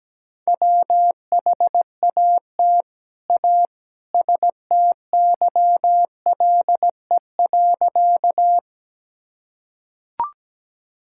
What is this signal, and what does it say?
Morse code, heard as a keyed tone, about 17 wpm, 700 Hz. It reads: WHAT A STYLE.
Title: Sentences from Top 1000 Words - Review Part 5 - 17wpm